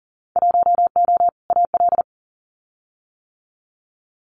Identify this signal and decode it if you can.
Morse code, heard as a keyed tone, about 40 wpm, 700 Hz. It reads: 1O AL
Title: Winter Field Day Exchanges 40wpm